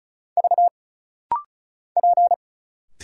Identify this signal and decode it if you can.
Morse code, heard as a keyed tone, about 36 wpm, 700 Hz. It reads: V P
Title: Single Letters - Rapid-Fire 35wpm